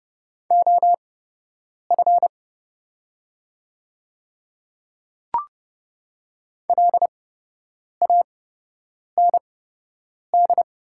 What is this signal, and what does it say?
Morse code, heard as a keyed tone, about 30 wpm, spaced as for 8 wpm, 700 Hz. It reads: OF LAND
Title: 8x Character Spacing - Top 300 Words - 30wpm